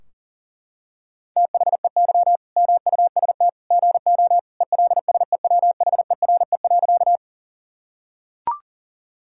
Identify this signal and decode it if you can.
Morse code, heard as a keyed tone, about 40 wpm, 700 Hz. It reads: THEY MUST GO ELSEWHERE.
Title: Sentences from Top 1600 Words 40wpm